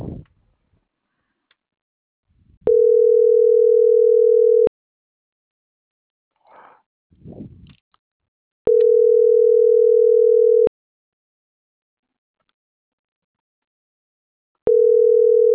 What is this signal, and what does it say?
Ringback tone (North America), 440 + 480 Hz.